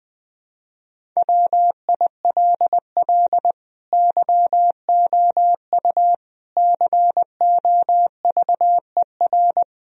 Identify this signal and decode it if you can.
Morse code, heard as a keyed tone, about 20 wpm, 700 Hz. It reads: WILL YOU COVER
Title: Sentences from Top 1000 Words - Review Part 1 - 20wpm